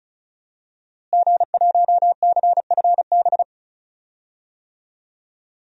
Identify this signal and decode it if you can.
Morse code, heard as a keyed tone, about 36 wpm, 700 Hz. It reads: G1CFB
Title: DXCC-100 Call Signs - 5 Characters - Repeated 2x - 35wpm